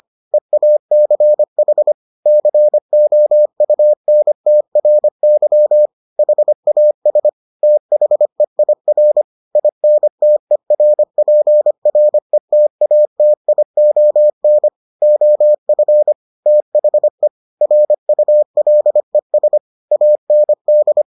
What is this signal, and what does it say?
Morse code, heard as a keyed tone, about 25 wpm, 600 Hz. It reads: EACH COUNTRY HAS THEIR INTERPRETATION OF THE RULES AND